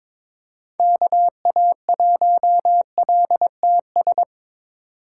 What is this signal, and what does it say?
Morse code, heard as a keyed tone, about 22 wpm, 700 Hz. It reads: KA1LTS
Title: USA Call Signs - 6 Characters - 3x Character Spacing - 22wpm